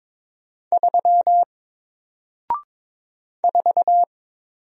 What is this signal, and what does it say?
Morse code, heard as a keyed tone, about 22 wpm, 700 Hz. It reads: 3 4
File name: Single-Letter-Number-22wpm